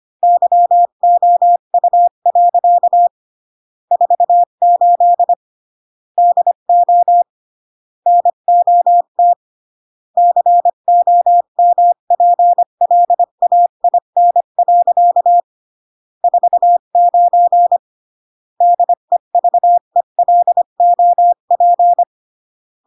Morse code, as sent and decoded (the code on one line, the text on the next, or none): -.-- --- ..- .-.-.- / ....- ---.. / -.. --- / -. --- - / -.-. --- -- .--. .-.. .- .. -. .-.-.- / ....- ----. / -.. ...- . .-.. --- .--.
YOU. 48 DO NOT COMPLAIN. 49 DEVELOP